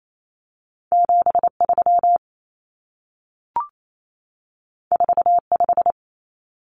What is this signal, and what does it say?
Morse code, heard as a keyed tone, about 28 wpm, 700 Hz. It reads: 73 45
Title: Numbers - 2-Digit 28wpm